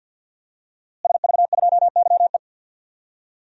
Morse code, heard as a keyed tone, about 46 wpm, 700 Hz. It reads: SV2YE
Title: DXCC-100 Call Signs - 5 Characters - Repeated 3x - 50wpm